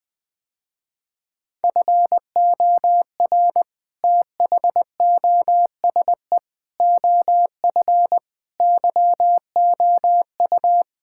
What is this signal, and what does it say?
Morse code, heard as a keyed tone, about 20 wpm, 700 Hz. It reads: FOR THOSE OF YOU
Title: Sentences with 7-Word Phrases - Spoken Beginning - 20wpm